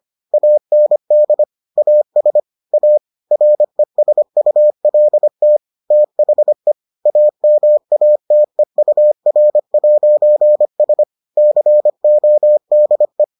Morse code, heard as a keyed tone, about 25 wpm, 600 Hz. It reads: AND AS A RESULT THE AMATEUR'S CODE